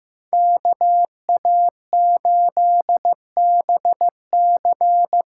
Morse code, heard as a keyed tone, about 15 wpm, 700 Hz. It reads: KA8BC
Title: USA Call Signs - 6 Characters - 6x Character Spacing - 15wpm